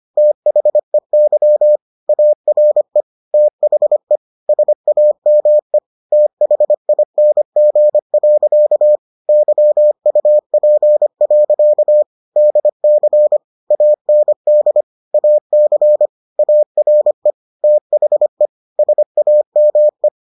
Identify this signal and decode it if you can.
Morse code, heard as a keyed tone, about 25 wpm, 600 Hz. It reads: THEY ARE THE SAME THING. YUP. DC AND AC ARE THE SAME